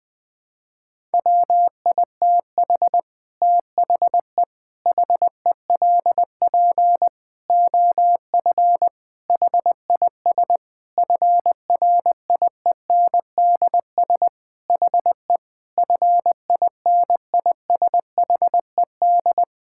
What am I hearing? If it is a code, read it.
Morse code, heard as a keyed tone, about 21 wpm, 700 Hz. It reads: WITH THE HELP OF HIS FRIENDS HE FINISHED
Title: Sentences with 4-Word Phrases - 20wpm